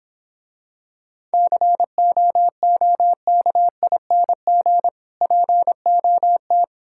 Morse code, heard as a keyed tone, about 26 wpm, 700 Hz. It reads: COOKING POT